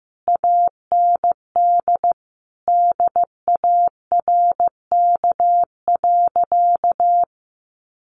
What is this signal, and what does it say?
Morse code, heard as a keyed tone, about 15 wpm, 700 Hz. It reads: AND DARK.
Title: Sentences from Top 500 Words 15wpm